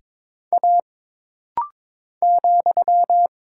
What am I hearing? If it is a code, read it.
Morse code, heard as a keyed tone, about 22 wpm, 700 Hz. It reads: A ,
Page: This rapid-fire timing is geared to push your ICR (Instant Character Recognition) skills.